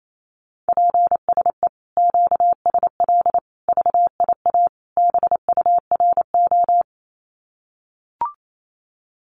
Morse code, heard as a keyed tone, about 28 wpm, 700 Hz. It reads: PSE QSL VIA BURO
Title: Common QSO Phrases 28wpm